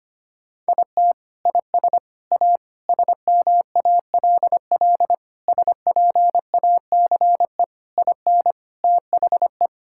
Morse code, heard as a keyed tone, about 25 wpm, 700 Hz. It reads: IT IS A SMALL SPACE IN THE
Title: Sentences from Top 1000 Words - Review Part 5 - 25wpm